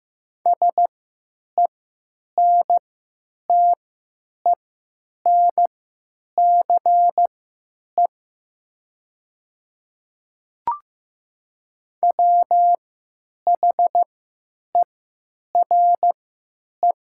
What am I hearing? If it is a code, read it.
Morse code, heard as a keyed tone, about 15 wpm, spaced as for 9 wpm, 700 Hz. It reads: SENTENCE WHERE
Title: 3x Character Spacing - Top 300 Words - 15wpm